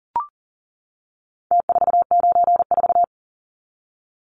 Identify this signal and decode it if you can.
Morse code, heard as a keyed tone, about 40 wpm, 700 Hz. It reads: T494
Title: Numbers - Contest Style - 4-Digit 40wpm